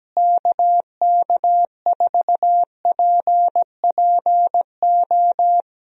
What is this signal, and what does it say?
Morse code, heard as a keyed tone, about 17 wpm, 700 Hz. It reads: KK4PPO
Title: USA Call Signs - 6 Characters - Repeated 2x - 17wpm